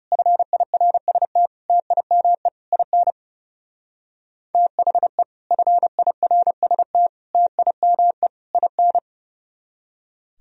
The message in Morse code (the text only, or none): FIRST TIME IN THE FIRST TIME IN